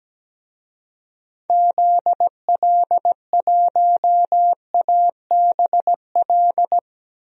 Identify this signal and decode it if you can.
Morse code, heard as a keyed tone, about 17 wpm, 700 Hz. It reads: ZL1ABL
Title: DXCC-100 Call Signs - 6 Characters - 17wpm